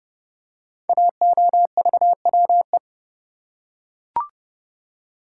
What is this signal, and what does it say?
Morse code, heard as a keyed tone, about 30 wpm, 700 Hz. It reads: AOVWE
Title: Code Groups - 5 Letters - 30wpm